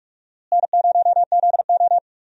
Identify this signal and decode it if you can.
Morse code, heard as a keyed tone, about 41 wpm, 700 Hz. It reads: N0ZO
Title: USA Call Signs - 4 Characters - Repeated 3x - 45wpm